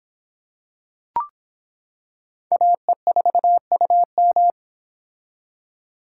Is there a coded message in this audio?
Yes: Morse code from 2.5 to 4.5 s, heard as a keyed tone, about 25 wpm, 700 Hz. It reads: AE4UM